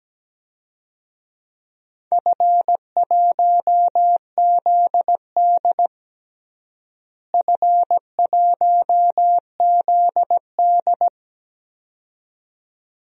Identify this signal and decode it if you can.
Morse code, heard as a keyed tone, about 17 wpm, 700 Hz. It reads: F1ZD F1ZD
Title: DXCC-100 Call Signs - 4 Characters - Repeated 2x - 17wpm